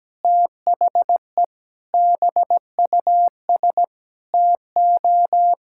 Morse code, heard as a keyed tone, about 17 wpm, 700 Hz. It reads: THE BUS TO
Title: Sentences from Top 1000 Words - Review Part 6 - 17wpm